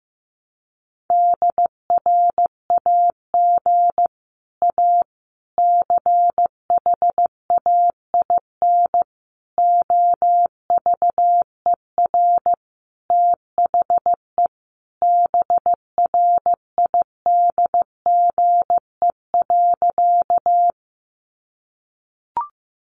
Morse code, heard as a keyed tone, about 15 wpm, 700 Hz. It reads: DRAG A CHAIN OVER THE BRIDGE.
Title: Sentences from Top 1500 Words 15wpm